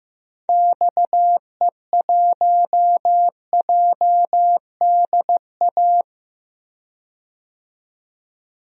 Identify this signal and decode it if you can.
Morse code, heard as a keyed tone, about 15 wpm, 700 Hz. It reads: XE1JDA
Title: DXCC-100 Call Signs - 6 Characters - 15wpm